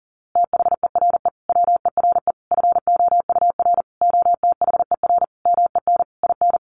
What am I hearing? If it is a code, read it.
Morse code, heard as a keyed tone, about 40 wpm, 700 Hz. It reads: THERE WERE FOUR OTHER MEN IN
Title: A Sound Of Thunder 40wpm